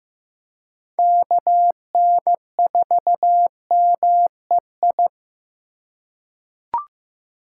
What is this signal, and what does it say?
Morse code, heard as a keyed tone, about 16 wpm, 700 Hz. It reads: KN4MEI
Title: USA Call Signs - 6 Characters - 4x Character Spacing - 15wpm